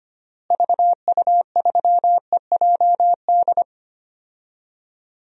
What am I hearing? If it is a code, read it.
Morse code, heard as a keyed tone, about 25 wpm, 700 Hz. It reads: VU3EJD
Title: DXCC-100 Call Signs - 6 Characters - Repeated 3x - 25wpm